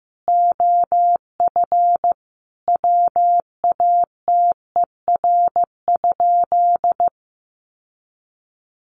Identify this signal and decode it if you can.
Morse code, heard as a keyed tone, about 15 wpm, 700 Hz. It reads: OF WATER?
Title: Sentences from Top 1300 Words 15wpm